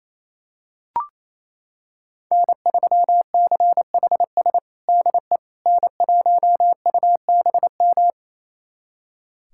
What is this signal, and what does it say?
Morse code, heard as a keyed tone, about 28 wpm, 700 Hz. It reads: N3CHS DE N1UBM
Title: USA Callsign DE Callsign - 5 Characters - 28wpm